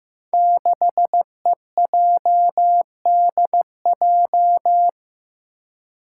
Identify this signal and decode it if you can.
Morse code, heard as a keyed tone, about 15 wpm, 700 Hz. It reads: 6EJDJ
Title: Five Challenging Letter-Number Combinations 15wpm